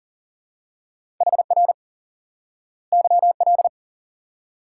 Morse code, heard as a keyed tone, about 40 wpm, 700 Hz. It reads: HR YL
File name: IQE Course 3 - Lesson 89 - YAGI - Two Elements - 40wpm